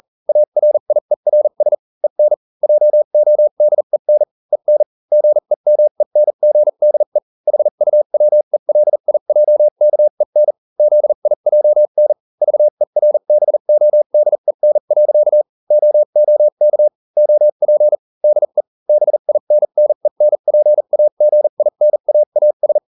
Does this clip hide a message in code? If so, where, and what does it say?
Yes: Morse code from 0.3 to 22.8 s, heard as a keyed tone, about 40 wpm, spaced as for 35 wpm, 600 Hz. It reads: ARIERS EN JODEN EN GEMENGDE HUWELIJKEN ZIJN VERBODEN. OOK OP DE BINNENPAGINAAS